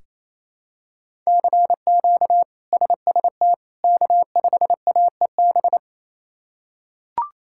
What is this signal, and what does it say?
Morse code, heard as a keyed tone, about 28 wpm, 700 Hz. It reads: CQ SST K5AEB
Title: K1USN SST QSO Practice - 28wpm